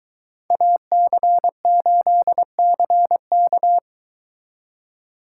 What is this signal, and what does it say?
Morse code, heard as a keyed tone, about 23 wpm, 700 Hz. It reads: AC8CK